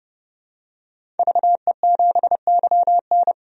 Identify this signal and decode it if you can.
Morse code, heard as a keyed tone, about 30 wpm, 700 Hz. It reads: VE7YN
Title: DXCC-100 Call Signs - 5 Characters - 30wpm